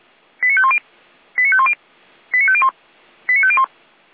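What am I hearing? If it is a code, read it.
Five-tone selective call (ZVEI), heard as quick runs of five tones, tones of about 70 ms. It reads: 89510 89510 89501 89501
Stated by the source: ZVEI five-tone selective call received on SDR.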